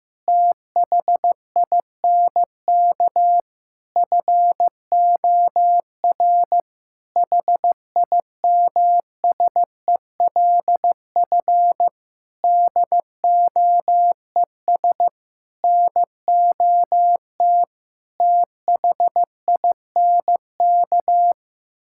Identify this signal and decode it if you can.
Morse code, heard as a keyed tone, about 15 wpm, 700 Hz. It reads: THINK FOR HIMSELF DOES NOT THINK
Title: Sentences from Top 1600 Words 15wpm